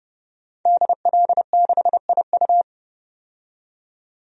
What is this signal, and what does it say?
Morse code, heard as a keyed tone, about 30 wpm, 700 Hz. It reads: DL6IU
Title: DXCC-100 Call Signs - 5 Characters - 30wpm